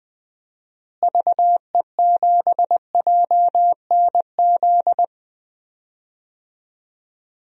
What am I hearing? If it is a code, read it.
Morse code, heard as a keyed tone, about 21 wpm, 700 Hz. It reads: VE7JNZ
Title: DXCC-100 Call Signs - 6 Characters - Repeated 2x - 20wpm